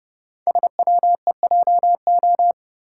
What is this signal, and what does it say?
Morse code, heard as a keyed tone, about 30 wpm, 690 Hz. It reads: SWEJO
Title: Code Groups - 5 Letters - 30wpm